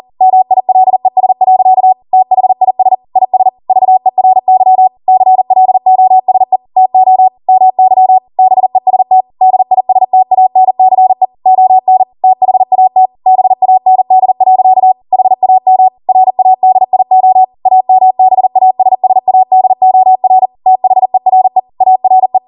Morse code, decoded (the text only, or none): MILES. THIS IS VERY CLOSE TO MY BEST DISTANCE ON THAT BAND. HAM RADIO AMBASSADOR THERE ARE